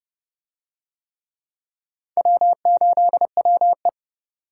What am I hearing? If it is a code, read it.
Morse code, heard as a keyed tone, about 28 wpm, 700 Hz. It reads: W8WE